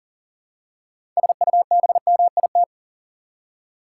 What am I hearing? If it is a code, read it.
Morse code, heard as a keyed tone, about 40 wpm, 700 Hz. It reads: SUBMIT
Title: Six Letter Words - 40wpm